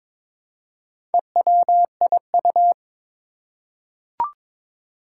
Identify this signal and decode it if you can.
Morse code, heard as a keyed tone, about 22 wpm, 700 Hz. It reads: EWIU
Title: Four Challenging Letter-Number Combinations 22wpm